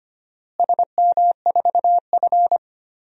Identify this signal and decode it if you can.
Morse code, heard as a keyed tone, about 25 wpm, 700 Hz. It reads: SM4F
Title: DXCC-100 Call Signs - 4 Characters - Repeated 3x - 25wpm